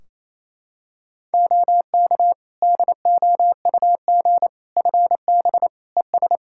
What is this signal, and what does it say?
Morse code, heard as a keyed tone, about 28 wpm, 700 Hz. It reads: OK DOUG FB ES
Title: Practice QSO - Exchange 2 - Info - 28wpm